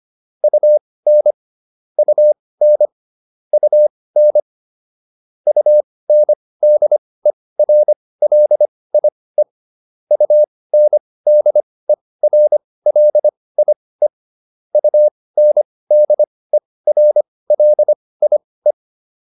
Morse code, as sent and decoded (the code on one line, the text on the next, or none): ..- -. / ..- -. / ..- -. / ..- -. -.. .-. .-.. .. / ..- -. -.. .-. .-.. .. / ..- -. -.. .-. .-.. .. .
UN UN UN UNDERLIE UNDERLIE UNDERLIE